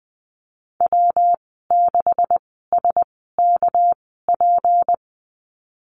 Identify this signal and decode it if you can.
Morse code, heard as a keyed tone, about 20 wpm, spaced as for 15 wpm, 700 Hz. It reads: W6SKP
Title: 219 Lesson 35g - Number 0 - Call Signs 20wpm